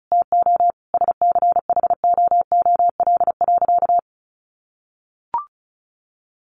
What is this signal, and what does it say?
Morse code, heard as a keyed tone, about 34 wpm, 700 Hz. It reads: TO SCHOOL.
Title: Easy Phrases 35wpm